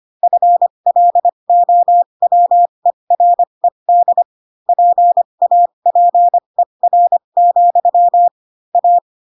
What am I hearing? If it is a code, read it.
Morse code, heard as a keyed tone, about 25 wpm, spaced as for 22 wpm, 700 Hz. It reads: FLOWERED PAPER, A